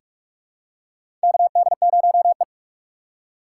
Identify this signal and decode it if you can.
Morse code, heard as a keyed tone, about 46 wpm, 700 Hz. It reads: KD0E
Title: USA Call Signs - 4 Characters - 2x Character Spacing - 45wpm